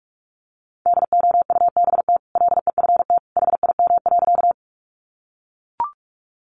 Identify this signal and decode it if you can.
Morse code, heard as a keyed tone, about 44 wpm, 700 Hz. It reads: DOUBT LEFT HIM.
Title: Sentences from Top 1500 Words 45wpm